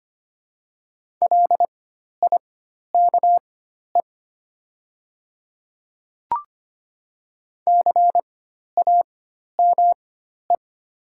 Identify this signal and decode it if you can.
Morse code, heard as a keyed tone, about 25 wpm, spaced as for 12 wpm, 700 Hz. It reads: LIKE CAME
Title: 4x Character Spacing - Top 200 Words - 25wpm